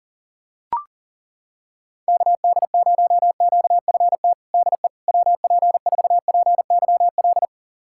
Morse code, heard as a keyed tone, about 40 wpm, 700 Hz. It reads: KD0QFT DE WP4PYL
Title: USA Callsign DE Callsign - 6 Characters - 40wpm